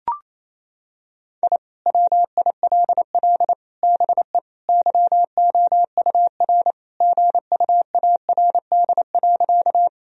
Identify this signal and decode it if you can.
Morse code, heard as a keyed tone, about 28 wpm, 700 Hz. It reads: I WILL BE YOUR GUARD.
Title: Sentences from Top 1000 Words - Review Part 4 - 28wpm